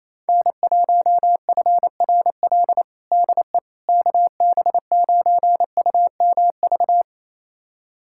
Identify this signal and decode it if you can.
Morse code, heard as a keyed tone, about 27 wpm, 700 Hz. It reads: N1FRL DE KB9UMV
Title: USA Callsign DE Callsign 28wpm